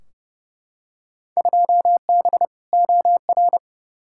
Morse code, heard as a keyed tone, about 31 wpm, 700 Hz. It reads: 2B OR
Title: Field Day Exchanges Encore - 30wpm